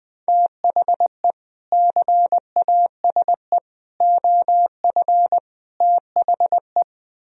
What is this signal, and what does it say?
Morse code, heard as a keyed tone, about 20 wpm, 700 Hz. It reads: THE CASE OF THE
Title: Speed-Racing - Sets of 4 Words 17wpm